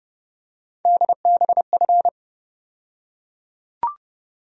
Morse code, heard as a keyed tone, about 30 wpm, 700 Hz. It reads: DBF
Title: USA Call Sign Postfixes - 3 Letters 30wpm